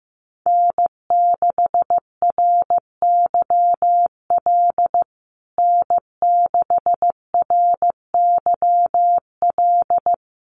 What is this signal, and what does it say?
Morse code, heard as a keyed tone, about 15 wpm, 700 Hz. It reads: N6RYL N6RYL
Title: USA Call Signs Repeated 3x 15wpm